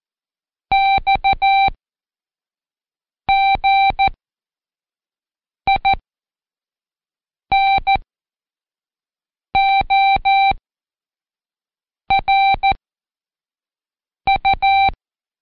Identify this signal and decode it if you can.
Morse code, heard as a keyed tone, about 14 wpm, 780 Hz. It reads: X G I N O R U
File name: Code Quick / AudioDownld1 / Seg74-Day8-Pract 1234